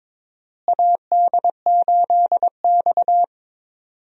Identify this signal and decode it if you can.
Morse code, heard as a keyed tone, about 22 wpm, 700 Hz. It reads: AD8X
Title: USA Call Signs - 4 Characters - Repeated 3x - 22wpm